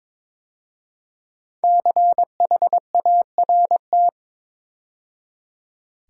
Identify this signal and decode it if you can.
Morse code, heard as a keyed tone, about 22 wpm, 700 Hz. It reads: CHART
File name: 5 Letter Words - First 3 Letters Given - 22wpm